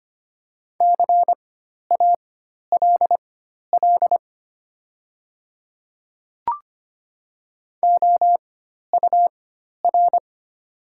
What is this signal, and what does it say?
Morse code, heard as a keyed tone, about 25 wpm, spaced as for 12 wpm, 700 Hz. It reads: CALL OUR
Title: 4x Character Spacing - Top 200 Words - 25wpm